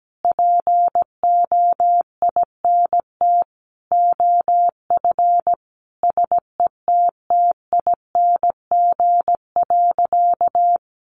Morse code, heard as a keyed tone, about 17 wpm, 700 Hz. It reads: POINT OF SETTING.
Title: Sentences from Top 1300 Words 17wpm